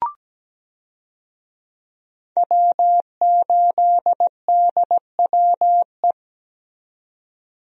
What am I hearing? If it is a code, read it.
Morse code, heard as a keyed tone, about 17 wpm, 700 Hz. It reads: W8DWE
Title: USA Call Signs - 5 Characters - 17wpm